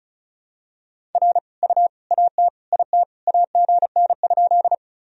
Morse code, heard as a keyed tone, about 36 wpm, 700 Hz. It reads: R U AT IT AGN?
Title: Sentences from Top 200 Words and Abbreviations 35wpm